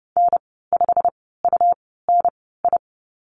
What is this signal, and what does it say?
Morse code, heard as a keyed tone, about 30 wpm, spaced as for 17 wpm, 700 Hz. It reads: N5UNI
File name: Lesson 19g - Letter w - Call Signs 30wpm